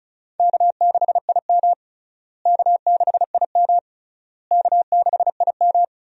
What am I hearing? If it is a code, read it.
Morse code, heard as a keyed tone, about 36 wpm, 700 Hz. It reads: K6IM K6IM K6IM
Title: USA Call Signs - 4 Characters - Repeated 3x - 35wpm